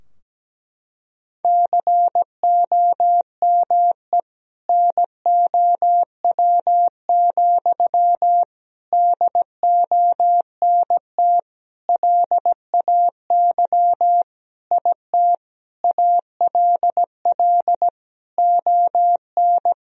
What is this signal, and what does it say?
Morse code, heard as a keyed tone, about 17 wpm, 700 Hz. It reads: COME NOW, DONT LAY IT ALL ON
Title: Sentences from Top 1000 Words - Review Part 2 - 17wpm